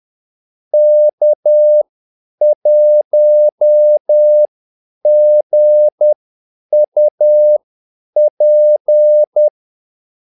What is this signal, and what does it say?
Morse code, heard as a keyed tone, about 10 wpm, spaced as for 8 wpm, 600 Hz. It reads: K1GUP
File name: Callsigns-10-8wpm